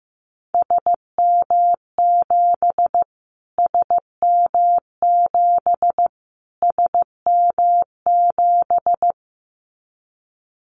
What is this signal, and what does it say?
Morse code, heard as a keyed tone, about 15 wpm, 700 Hz. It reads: SM7 SM7 SM7
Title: Callsign Prefixes for Top 25 DXCC - Repeated 3x - 15wpm